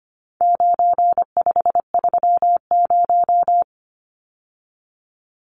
Morse code, heard as a keyed tone, about 25 wpm, 700 Hz. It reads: 9530
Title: Numbers - 4-Digit 25wpm